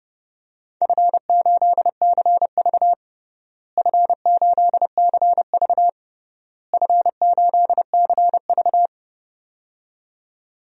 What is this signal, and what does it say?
Morse code, heard as a keyed tone, about 30 wpm, 700 Hz. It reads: F8CV F8CV F8CV
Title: DXCC-100 Call Signs - 4 Characters - Repeated 3x - 30wpm